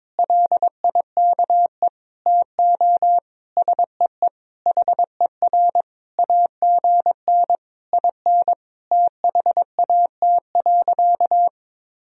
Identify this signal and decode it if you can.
Morse code, heard as a keyed tone, about 22 wpm, 700 Hz. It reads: LIKE TO SEE HER AGN IN THAT.
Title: Sentences from Top 200 Words and Abbreviations 22wpm